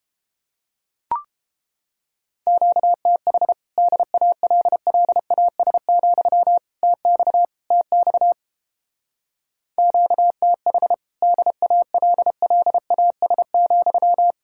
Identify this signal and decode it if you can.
Morse code, heard as a keyed tone, about 30 wpm, 700 Hz. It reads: QTH DALLAS, TX TX QTH DALLAS,